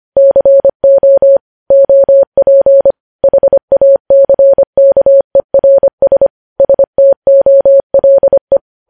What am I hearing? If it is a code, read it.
Morse code, heard as a keyed tone, about 25 wpm, 550 Hz. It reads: CO OP HACKERS STOLE
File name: podcast / News headlines-1x-25x25